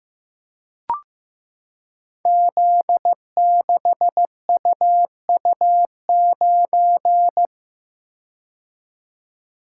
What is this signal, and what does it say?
Morse code, heard as a keyed tone, about 15 wpm, 700 Hz. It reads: Z6UU9
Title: Code Groups - 5 Letter:Numbers - 15wpm